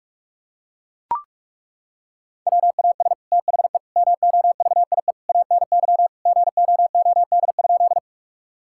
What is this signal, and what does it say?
Morse code, heard as a keyed tone, about 41 wpm, 700 Hz. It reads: WAS THE MOVIE ANY GOOD?
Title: Sentences from Top 1000 Words - Review Part 5 - 45wpm